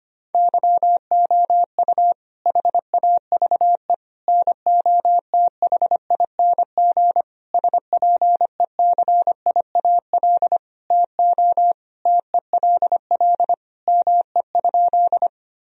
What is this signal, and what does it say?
Morse code, heard as a keyed tone, about 25 wpm, 700 Hz. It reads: YOU HAVE NOTHING SPECIAL TO TELL ME?
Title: Sentences from Top 1000 Words - Review Part 3 - 25wpm